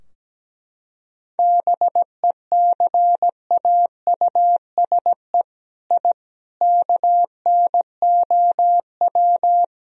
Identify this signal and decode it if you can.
Morse code, heard as a keyed tone, about 17 wpm, 700 Hz. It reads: BECAUSE I KNOW
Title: Sentences from Top 1000 Words - Review Part 3 - 17wpm